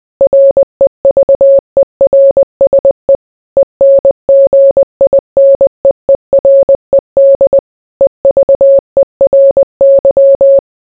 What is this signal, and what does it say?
Morse code, heard as a keyed tone, about 20 wpm, 550 Hz. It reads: REVERSE ENGINEERED EVERY